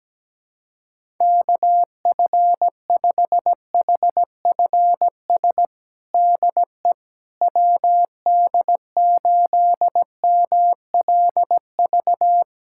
Morse code, heard as a keyed tone, about 17 wpm, 700 Hz. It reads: KF5HFS DE WD8MLV
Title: USA Callsign DE Callsign - 6 Characters - 17wpm